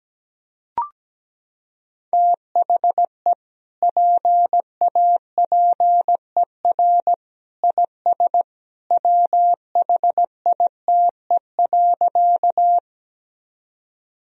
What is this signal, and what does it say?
Morse code, heard as a keyed tone, about 17 wpm, 700 Hz. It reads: THE PAPER IS WHITE.
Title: Sentences from Top 1000 Words - Review Part 1 - 17wpm